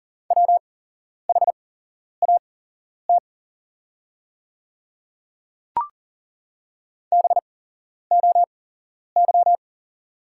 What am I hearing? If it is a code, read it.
Morse code, heard as a keyed tone, about 40 wpm, spaced as for 11 wpm, 700 Hz. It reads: WHAT BOY
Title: 8x Character Spacing - Top 500 Words - 40wpm